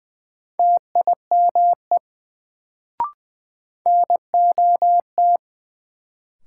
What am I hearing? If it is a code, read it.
Morse code, heard as a keyed tone, about 20 wpm, 700 Hz. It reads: TIME NOT
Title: Sets of 2 Words - New Word Drill - 20wpm